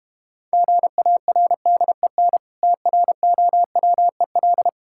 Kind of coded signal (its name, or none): Morse code